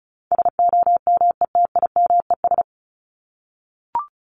Morse code, heard as a keyed tone, about 34 wpm, 700 Hz. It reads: SOMETIMES
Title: Top 501-750 Words 35wpm